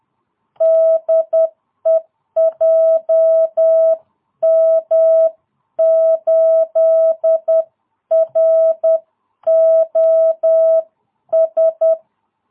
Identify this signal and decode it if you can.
Morse code, heard as a keyed tone, about 10 wpm, 660 Hz. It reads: DE JM8ROS